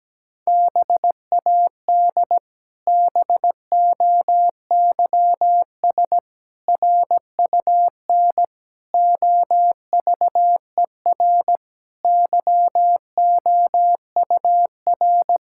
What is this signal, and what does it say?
Morse code, heard as a keyed tone, about 17 wpm, 700 Hz. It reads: BAD BOYS RUN OVER YOUR
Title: Pangram Challenge v2 - 17wpm